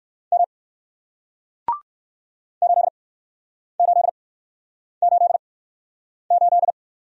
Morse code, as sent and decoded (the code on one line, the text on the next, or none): -- / ---.. ---.. ---.. ---..
M 8888